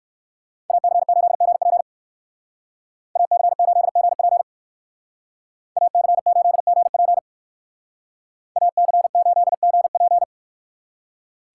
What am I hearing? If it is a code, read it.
Morse code, heard as a keyed tone, about 55 wpm, 700 Hz. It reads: AC8GP AC8GP AC8GP AC8GP
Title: Speed Racing - USA 2x2 Call Signs 45wpm